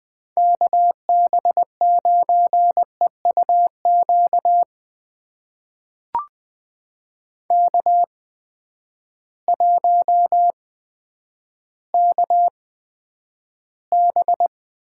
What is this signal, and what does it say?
Morse code, heard as a keyed tone, about 20 wpm, spaced as for 10 wpm, 700 Hz. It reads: KB9EUQ K1KB
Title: USA Call Signs - 8x Character Spacing - 20wpm